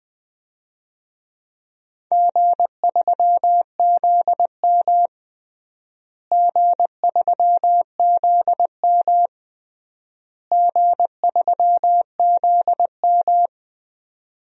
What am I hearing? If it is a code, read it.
Morse code, heard as a keyed tone, about 21 wpm, 700 Hz. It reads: G3ZM G3ZM G3ZM
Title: DXCC-100 Call Signs - 4 Characters - Repeated 3x - 20wpm